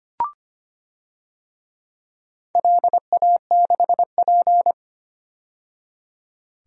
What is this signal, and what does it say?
Morse code, heard as a keyed tone, about 25 wpm, 700 Hz. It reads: LA6P
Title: DXCC-100 Call Signs - 4 Characters - 25wpm